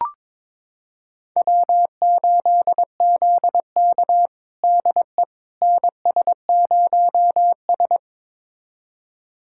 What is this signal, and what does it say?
Morse code, heard as a keyed tone, about 22 wpm, 700 Hz. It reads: W8ZK DE NS0S
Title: USA Callsign DE Callsign - 4 Characters - 22wpm